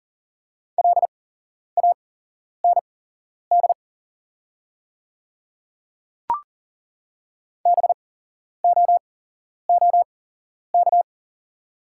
Morse code, heard as a keyed tone, about 40 wpm, spaced as for 11 wpm, 690 Hz. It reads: LAND BOOK
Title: 8x Character Spacing - Top 500 Words - 40wpm